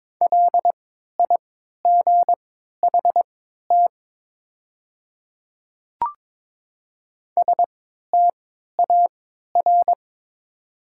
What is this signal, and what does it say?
Morse code, heard as a keyed tone, about 22 wpm, spaced as for 12 wpm, 700 Hz. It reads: LIGHT STAR
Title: 3x Character Spacing - Top 500 Words - 22wpm